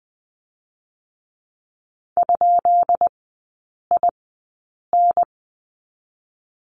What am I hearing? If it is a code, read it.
Morse code, heard as a keyed tone, about 20 wpm, spaced as for 8 wpm, 700 Hz. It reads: ?IN
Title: 152 Lesson 25e - Letter p - Three Characters 20wpm